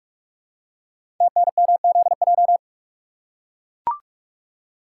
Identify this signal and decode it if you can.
Morse code, heard as a keyed tone, about 44 wpm, 690 Hz. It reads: TNMZJ
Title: Code Groups - 5 Letters - 45wpm